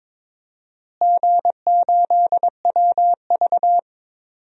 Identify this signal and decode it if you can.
Morse code, heard as a keyed tone, about 21 wpm, 700 Hz. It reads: G8WV
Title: DXCC-100 Call Signs - 4 Characters - Repeated 3x - 22wpm